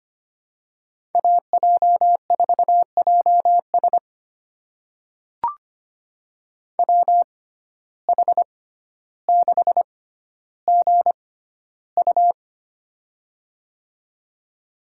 Morse code, heard as a keyed tone, about 25 wpm, spaced as for 13 wpm, 700 Hz. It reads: AJ4JS WH6GU
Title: USA Call Signs - 5 Characters - 6x Character Spacing - 25wpm